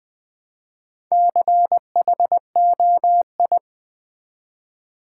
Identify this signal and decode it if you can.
Morse code, heard as a keyed tone, about 20 wpm, 700 Hz. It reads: CHOI
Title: Build A Word - 1 Letter At A Time - 3 to 5-Letter Words - 20wpm